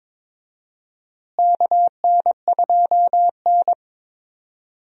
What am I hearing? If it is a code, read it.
Morse code, heard as a keyed tone, about 22 wpm, 700 Hz. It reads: KN2N
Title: USA Call Signs - 4 Characters - Repeated 3x - 22wpm